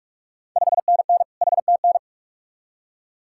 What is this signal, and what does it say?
Morse code, heard as a keyed tone, about 46 wpm, 690 Hz. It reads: 5NN HTN
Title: Ontario QSO Party Exchange 45wpm